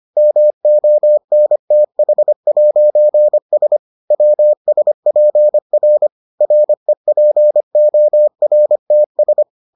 Morse code, heard as a keyed tone, about 25 wpm, 600 Hz. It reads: MONTH'S WSPR REPORTS